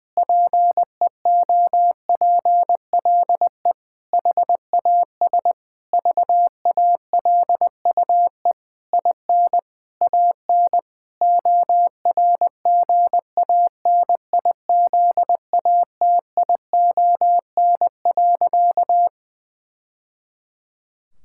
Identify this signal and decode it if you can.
Morse code, heard as a keyed tone, about 21 wpm, 700 Hz. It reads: PEOPLE HAS VALUE IN AN ORGANIZATION.
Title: Sentences from Top 1000 Words - Review Part 6 - 20wpm